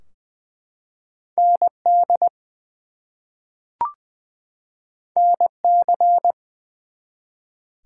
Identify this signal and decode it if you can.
Morse code, heard as a keyed tone, about 20 wpm, 700 Hz. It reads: ND NC
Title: US States and Canadian Provinces 20wpm